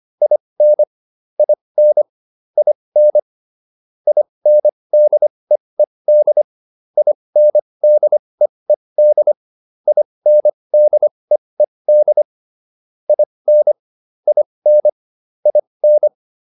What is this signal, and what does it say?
Morse code, heard as a keyed tone, about 25 wpm, spaced as for 20 wpm, 600 Hz. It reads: IN IN IN INDEED INDEED INDEED IN IN IN